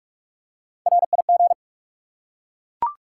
Morse code, heard as a keyed tone, about 44 wpm, 700 Hz. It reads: RIG
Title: Top 180 QSO Elements 45wpm